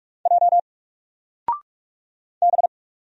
Morse code, heard as a keyed tone, about 44 wpm, 700 Hz. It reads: J B
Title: Single Character (Letter:Number:Punctuation:Prosign) Alt 45wpm